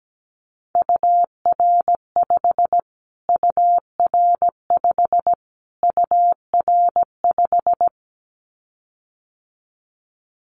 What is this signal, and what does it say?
Morse code, heard as a keyed tone, about 17 wpm, 700 Hz. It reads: UR5 UR5 UR5
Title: Callsign Prefixes for Top 25 DXCC - Repeated 3x - 17wpm